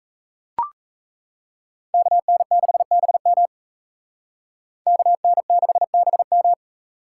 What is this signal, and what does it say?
Morse code, heard as a keyed tone, about 35 wpm, 700 Hz. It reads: KN6BM KN6BM
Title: Speed Racing - USA 2x2 Call Signs 28wpm